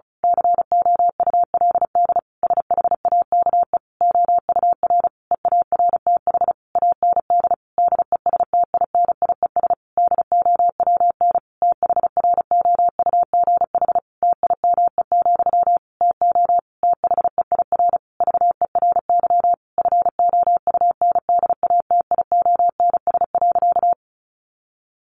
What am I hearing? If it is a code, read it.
Morse code, heard as a keyed tone, about 34 wpm, 700 Hz. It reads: COULD SHAKE OUR EARTH AND DESTINIES DOWN THROUGH TIME, TO THEIR VERY FOUNDATIONS.
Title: A Sound Of Thunder 35wpm